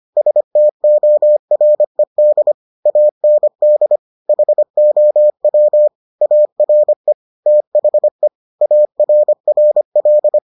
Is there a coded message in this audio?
Yes: Morse code from 0.2 to 10.4 s, heard as a keyed tone, about 25 wpm, 600 Hz. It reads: STORED AND HOW ARE THE ARRL